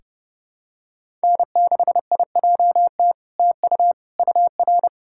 Morse code, heard as a keyed tone, about 30 wpm, 700 Hz. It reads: N6IJT TU UR
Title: SOTA QSO Practice 30wpm